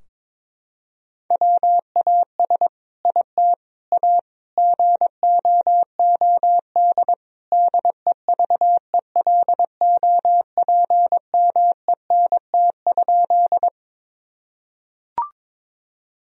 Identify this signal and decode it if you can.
Morse code, heard as a keyed tone, about 22 wpm, 700 Hz. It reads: WAS IT A GOOD DEVELOPMENT?
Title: Sentences from Top 1000 Words - Review Part 4 - 22wpm